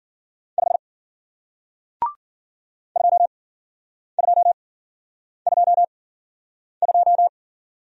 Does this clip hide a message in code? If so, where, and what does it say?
Yes: Morse code from 0.6 to 7.3 s, heard as a keyed tone, about 53 wpm, spaced as for 9 wpm, 700 Hz. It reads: 5 2222